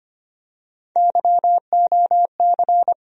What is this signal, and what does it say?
Morse code, heard as a keyed tone, about 25 wpm, 700 Hz. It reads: YOC
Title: USA Call Sign Postfixes - 3 Letters 25wpm